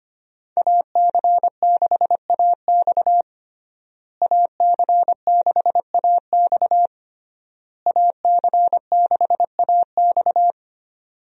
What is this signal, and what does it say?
Morse code, heard as a keyed tone, about 25 wpm, 700 Hz. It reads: AC6AX AC6AX AC6AX
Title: USA Call Signs - 5 Characters - Repeated 3x - 25wpm